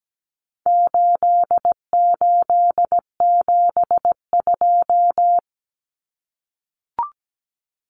Morse code, heard as a keyed tone, about 17 wpm, 700 Hz. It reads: 8872
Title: Numbers - 4-Digit 17wpm